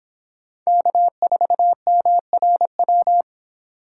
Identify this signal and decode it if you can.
Morse code, heard as a keyed tone, about 25 wpm, 700 Hz. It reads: K4MRW